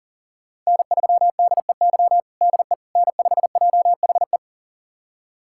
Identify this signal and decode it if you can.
Morse code, heard as a keyed tone, about 40 wpm, 700 Hz. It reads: N3DEY DE N5JHE
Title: USA Callsign DE Callsign - 5 Characters - 40wpm